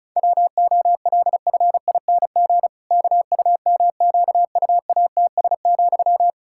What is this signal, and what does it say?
Morse code, heard as a keyed tone, about 36 wpm, 700 Hz. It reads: WOLFING KUMQUATS,
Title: Pangram Challenge v2 - 35wpm